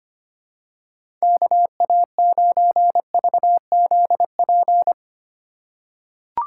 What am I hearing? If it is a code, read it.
Morse code, heard as a keyed tone, about 25 wpm, 700 Hz. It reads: KA9VZP
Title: USA Call Signs - 4x Character Spacing - 25wpm